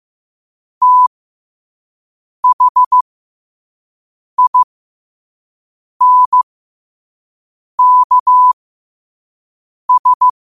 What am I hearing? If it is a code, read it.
Morse code, heard as a keyed tone, about 15 wpm, spaced as for 5 wpm, 1000 Hz. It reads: THINKS
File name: qotdfiles / 5wpm